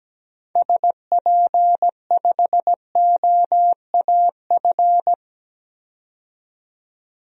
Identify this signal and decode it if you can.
Morse code, heard as a keyed tone, about 17 wpm, 700 Hz. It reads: SP5OAF
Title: DXCC-100 Call Signs - 6 Characters - 17wpm